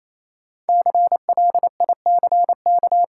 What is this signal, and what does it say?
Morse code, heard as a keyed tone, about 28 wpm, 700 Hz. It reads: CLICK
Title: Sentences from Top 1100 Words - New Word Drill - 28wpm